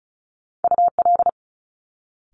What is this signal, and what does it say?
Morse code, heard as a keyed tone, about 36 wpm, 700 Hz. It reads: UL
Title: Top Two Letter Combinations in Words 35wpm